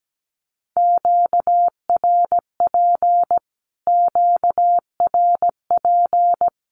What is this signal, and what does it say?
Morse code, heard as a keyed tone, about 17 wpm, 700 Hz. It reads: QRP QRP
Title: Q-Codes Repeated 3x 17wpm